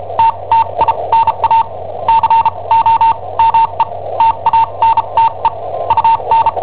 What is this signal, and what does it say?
Morse code, heard as a keyed tone, about 34 wpm, spaced as for 27 wpm, 910 Hz. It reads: TTINA COME TANTE UN
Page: Cassetta esercizi preparata da I7OHP per IW5ELP